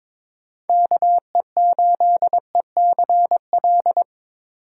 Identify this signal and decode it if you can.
Morse code, heard as a keyed tone, about 26 wpm, 700 Hz. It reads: KE8ECL
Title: Speed Racing - USA 2x3 Call Signs 22wpm